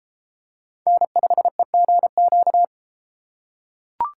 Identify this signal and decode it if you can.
Morse code, heard as a keyed tone, about 29 wpm, 700 Hz. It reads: N5EGQ